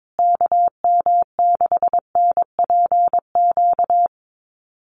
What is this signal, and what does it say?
Morse code, heard as a keyed tone, about 22 wpm, 700 Hz. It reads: KM6NPQ
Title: USA Call Signs Repeated 3x 22wpm